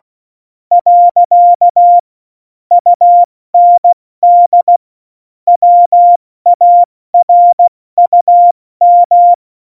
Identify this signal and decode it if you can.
Morse code, heard as a keyed tone, about 16 wpm, 700 Hz. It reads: . UND WARUM
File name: Morsekurs / Witze / Witz 12. Juni 2018, 70 F80